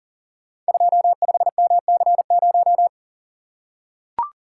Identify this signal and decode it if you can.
Morse code, heard as a keyed tone, about 40 wpm, 690 Hz. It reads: 25MC0
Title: Code Groups - 5 Letter:Numbers - 40wpm